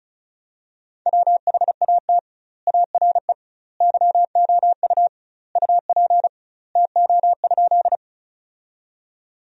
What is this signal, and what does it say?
Morse code, heard as a keyed tone, about 36 wpm, 700 Hz. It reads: WHAT ARE YOU UP TO?
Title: 2x Word-Spacing - Sentences from Top 100 Words - 35wpm